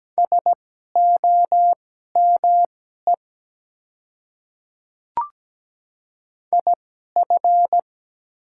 Morse code, heard as a keyed tone, about 17 wpm, spaced as for 12 wpm, 700 Hz. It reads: SOME IF
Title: 2x Character Spacing - Top 200 Words - 17wpm